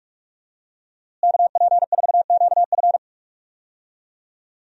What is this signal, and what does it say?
Morse code, heard as a keyed tone, about 46 wpm, 700 Hz. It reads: KP4QF
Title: DXCC-100 Call Signs - 5 Characters - Repeated 2x - 45wpm